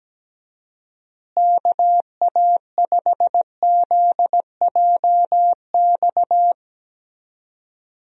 Morse code, heard as a keyed tone, about 17 wpm, 700 Hz. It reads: KA5ZJX
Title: USA Call Signs - 6 Characters - Repeated 3x - 17wpm